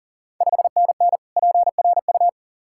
Morse code, heard as a keyed tone, about 40 wpm, 690 Hz. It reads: 5NN PRU
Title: Ontario QSO Party Exchange 40wpm